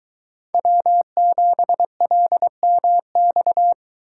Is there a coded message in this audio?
Yes: Morse code from 0.5 to 3.7 s, heard as a keyed tone, about 23 wpm, 700 Hz. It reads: W7LMX